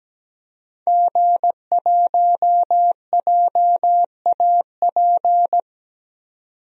Morse code, heard as a keyed tone, about 17 wpm, 700 Hz. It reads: G1JAP
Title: DXCC-100 Call Signs - 5 Characters - 17wpm